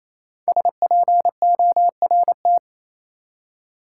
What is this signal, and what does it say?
Morse code, heard as a keyed tone, about 28 wpm, 700 Hz. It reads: SPORT
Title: Five Letter Words - 28wpm